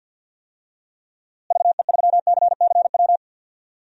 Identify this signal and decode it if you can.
Morse code, heard as a keyed tone, about 51 wpm, 700 Hz. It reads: VE3CCW
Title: DXCC-100 Call Signs - 6 Characters - Repeated 3x - 50wpm